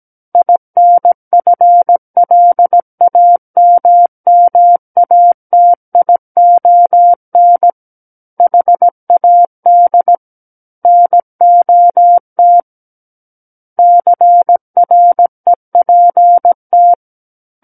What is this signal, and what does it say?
Morse code, heard as a keyed tone, about 17 wpm, 700 Hz. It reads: INFLAMMATION HAD NOT CREPT
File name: Book One / 08-Chapter 8 - 17 WPM